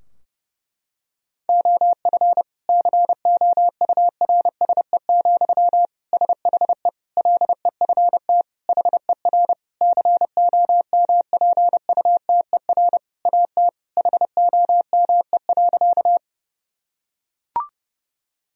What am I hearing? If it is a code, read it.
Morse code, heard as a keyed tone, about 30 wpm, 700 Hz. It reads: OF COURSE, SHE LEFT HER COMPUTER AT HOME.
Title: Sentences from Top 1000 Words - Review Part 3 - 30wpm